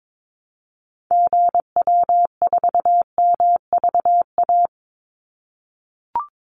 Morse code, heard as a keyed tone, about 22 wpm, 700 Hz. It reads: GW4MVA
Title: Call Signs - 22wpm